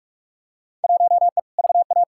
Morse code, heard as a keyed tone, about 48 wpm, 700 Hz. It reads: 1E VA
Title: Field Day Exchanges - 45wpm